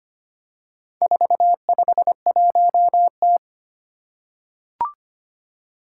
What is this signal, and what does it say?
Morse code, heard as a keyed tone, about 25 wpm, 700 Hz. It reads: 451T
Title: Four Challenging Letter-Number Combinations 25wpm